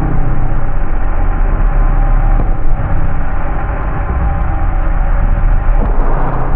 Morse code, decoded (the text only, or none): none